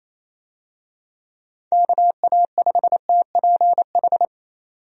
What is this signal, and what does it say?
Morse code, heard as a keyed tone, about 27 wpm, 700 Hz. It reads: KA5TPH
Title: USA Call Signs - 6 Characters - 28wpm